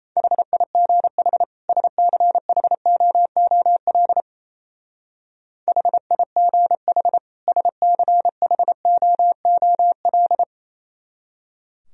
Morse code, heard as a keyed tone, about 33 wpm, 700 Hz. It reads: HIGH SCHOOL HIGH SCHOOL
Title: Speed-Racing - Sets of 2 Words 28wpm